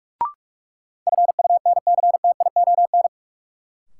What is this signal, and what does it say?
Morse code, heard as a keyed tone, about 46 wpm, 700 Hz. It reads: FUNCTION
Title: Sentences from Top 1100 Words - New Word Drill - 45wpm